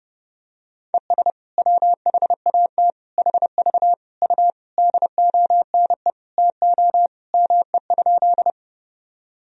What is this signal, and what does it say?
Morse code, heard as a keyed tone, about 32 wpm, 700 Hz. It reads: ES WHAT HV U DONE TO ME?
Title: Sentences from Top 500 Words and Abbreviations 30wpm